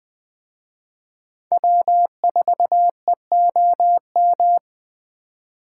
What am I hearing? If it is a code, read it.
Morse code, heard as a keyed tone, about 20 wpm, 700 Hz. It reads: W4EOM
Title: Speed Racing - USA 1x3 Call Signs 17wpm